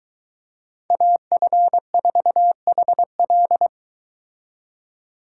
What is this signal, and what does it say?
Morse code, heard as a keyed tone, about 23 wpm, 700 Hz. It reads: AF4HL